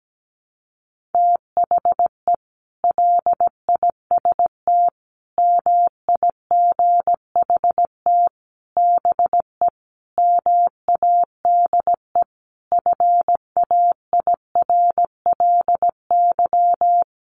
Morse code, heard as a keyed tone, about 17 wpm, 700 Hz. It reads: THE LIST MIGHT BE MADE FAIRLY
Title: Sentences from Top 1100 Words 17wpm